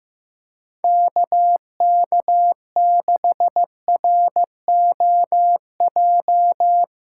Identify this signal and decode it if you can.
Morse code, heard as a keyed tone, about 16 wpm, 700 Hz. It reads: KK6ROJ
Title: USA Callsign DE Callsign - 6 Characters - 15wpm